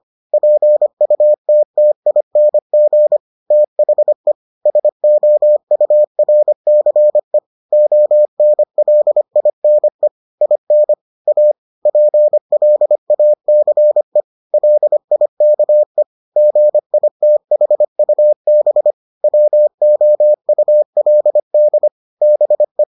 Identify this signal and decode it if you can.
Morse code, heard as a keyed tone, about 25 wpm, 600 Hz. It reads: PUTTING THE SOURCE ONLINE IN A PLACE LIKE GITHUB WOULD BE